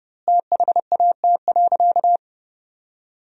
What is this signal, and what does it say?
Morse code, heard as a keyed tone, about 32 wpm, 700 Hz. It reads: THAT.
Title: 2x Word-Spacing - Sentences from Top 300 Words - 30wpm